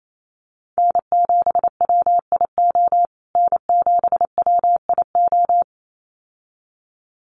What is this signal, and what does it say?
Morse code, heard as a keyed tone, about 28 wpm, 700 Hz. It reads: N7WIO N7WIO
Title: USA Call Signs Repeated 2x 28wpm